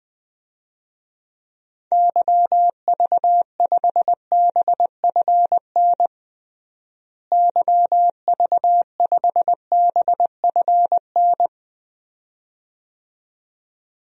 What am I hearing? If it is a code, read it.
Morse code, heard as a keyed tone, about 20 wpm, 700 Hz. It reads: YV5BFN YV5BFN
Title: DXCC-100 Call Signs - 6 Characters - Repeated 2x - 20wpm